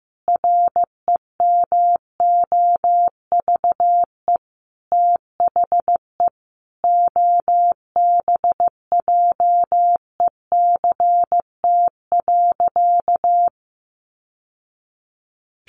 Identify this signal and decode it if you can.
Morse code, heard as a keyed tone, about 15 wpm, 700 Hz. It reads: REMOVE THE OBJECT.
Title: Sentences from Top 800 Words 15wpm